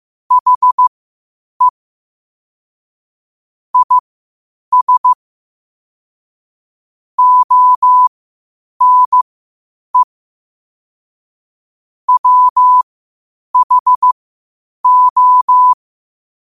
Morse code, heard as a keyed tone, about 15 wpm, spaced as for 8 wpm, 1000 Hz. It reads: HE IS ONE WHO
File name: qotdfiles / 8wpm